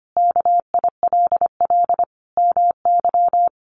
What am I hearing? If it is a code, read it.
Morse code, heard as a keyed tone, about 25 wpm, 700 Hz. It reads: KILL MY
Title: Sentences from Top 1400 Words 25wpm